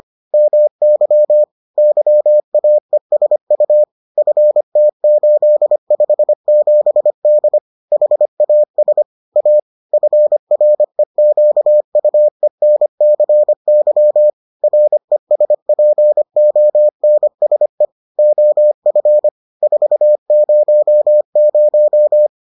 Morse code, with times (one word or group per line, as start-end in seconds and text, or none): 0.3-1.4 MY
1.8-3.8 YAESU
4.2-7.6 FT857D
7.9-9.0 HAS
9.4-9.6 A
9.9-14.3 FREQUENCY
14.6-17.9 RESPONSE
18.2-19.3 OF
19.6-22.3 400